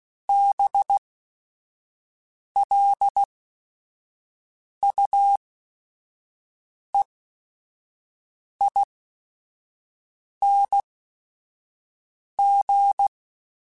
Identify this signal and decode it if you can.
Morse code, heard as a keyed tone, about 16 wpm, spaced as for 5 wpm, 800 Hz. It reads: BLUEING
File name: Special INGs / Special INGs-mp3s / Sngl Ing @5